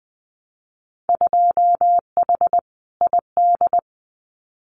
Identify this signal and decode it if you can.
Morse code, heard as a keyed tone, about 20 wpm, 700 Hz. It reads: 2H ID
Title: Winter Field Day Exchanges 20wpm